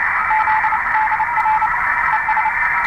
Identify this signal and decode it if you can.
Morse code, heard as a keyed tone, about 29 wpm, 870 Hz, with radio noise. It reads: ZDR ES